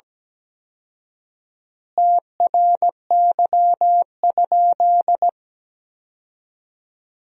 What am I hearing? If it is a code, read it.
Morse code, heard as a keyed tone, about 17 wpm, 700 Hz. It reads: TRY?
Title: 4x Word-Spacing - Sentences from Top 200 Words - 17wpm